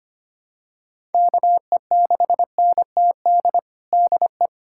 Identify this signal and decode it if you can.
Morse code, heard as a keyed tone, about 25 wpm, 700 Hz. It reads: KE6NTD DE
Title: USA Callsign DE Callsign 25wpm